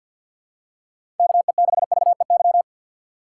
Morse code, heard as a keyed tone, about 53 wpm, 700 Hz. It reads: KE6VEY
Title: USA Call Signs - 6 Characters - 6x Character Spacing - 50wpm